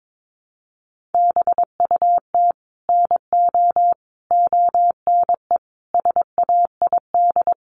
Morse code, heard as a keyed tone, about 22 wpm, 700 Hz. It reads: BUT NO ONE SAID
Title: Sentences from Top 1000 Words 22wpm